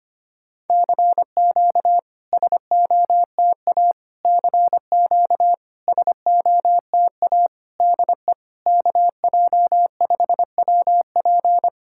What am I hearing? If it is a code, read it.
Morse code, heard as a keyed tone, about 25 wpm, 690 Hz. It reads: CQ SOTA CQ SOTA DE KJ5WP
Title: SOTA QSO Practice 25wpm